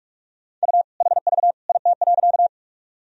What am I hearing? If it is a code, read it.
Morse code, heard as a keyed tone, about 46 wpm, 700 Hz. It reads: U HV IT.
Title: Sentences from Top 100 Words and Abbreviations 45wpm